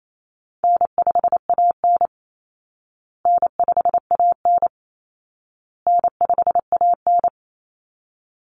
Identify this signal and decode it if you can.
Morse code, heard as a keyed tone, about 28 wpm, 700 Hz. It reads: N5AN N5AN N5AN
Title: Call Signs Repeated 2x - 28wpm